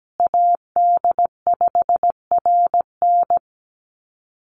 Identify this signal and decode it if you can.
Morse code, heard as a keyed tone, about 17 wpm, 700 Hz. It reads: AD5RN
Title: USA Call Signs Repeated 2x 17wpm